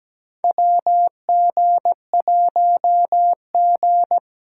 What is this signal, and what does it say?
Morse code, heard as a keyed tone, about 17 wpm, 700 Hz. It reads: WG1G
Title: USA Call Signs - 4 Characters - 6x Character Spacing - 17wpm